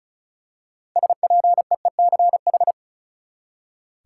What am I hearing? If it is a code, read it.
Morse code, heard as a keyed tone, about 34 wpm, 700 Hz. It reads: SPEECH
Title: Six Letter Words - 35wpm